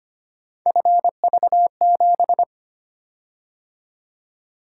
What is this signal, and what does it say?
Morse code, heard as a keyed tone, about 25 wpm, 700 Hz. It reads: FV7
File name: Three Letter-Number Combinations - 25wpm